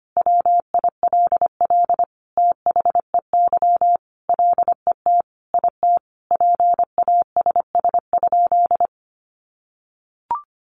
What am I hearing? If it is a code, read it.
Morse code, heard as a keyed tone, about 25 wpm, 700 Hz. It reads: WILL THEY LET IT PASS?
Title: Sentences from Top 400 Words 25wpm